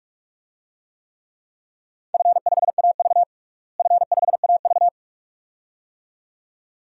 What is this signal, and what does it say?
Morse code, heard as a keyed tone, about 43 wpm, 700 Hz. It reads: F5AV F5AV
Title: DXCC-100 Call Signs - 4 Characters - Repeated 2x - 45wpm